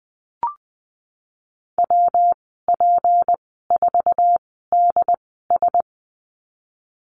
Morse code, heard as a keyed tone, about 20 wpm, spaced as for 15 wpm, 700 Hz. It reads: WP4DS
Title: 231 Lesson 37g - Letter x - Call Signs 20wpm